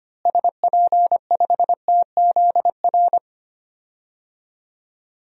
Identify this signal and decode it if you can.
Morse code, heard as a keyed tone, about 25 wpm, 700 Hz. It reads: SP5TZR
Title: DXCC-100 Call Signs - 6 Characters - Repeated 3x - 25wpm